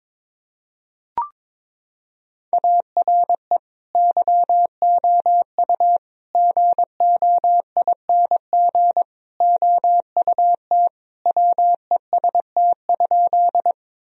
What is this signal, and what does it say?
Morse code, heard as a keyed tone, about 22 wpm, 700 Hz. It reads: ARE YOU GOING OUT WEST?
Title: Sentences from Top 1000 Words - Review Part 2 - 22wpm